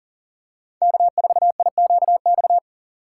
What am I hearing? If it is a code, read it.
Morse code, heard as a keyed tone, about 40 wpm, 700 Hz. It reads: K4IQX
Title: USA Call Signs - 6x Character Spacing - 40wpm